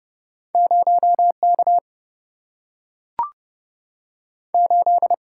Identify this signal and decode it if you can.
Morse code, heard as a keyed tone, about 30 wpm, 700 Hz. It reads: 0K 8
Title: Single-Letter Single-Number 30wpm